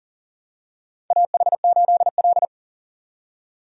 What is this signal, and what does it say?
Morse code, heard as a keyed tone, about 40 wpm, 700 Hz. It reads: AH8L
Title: USA Call Signs - 4 Characters - 40wpm